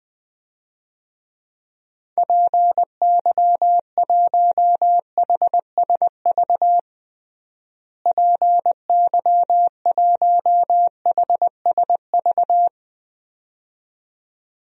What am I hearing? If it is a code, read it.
Morse code, heard as a keyed tone, about 20 wpm, 700 Hz. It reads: PY1HSV PY1HSV
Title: DXCC-100 Call Signs - 6 Characters - Repeated 2x - 20wpm